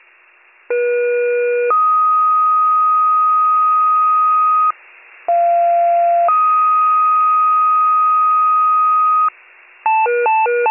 Two-tone page, heard as a one-second tone, then a three-second tone, about 480 then 1250 Hz, about 690 then 1250 Hz. It is heard from 0.7 to 9.3 s.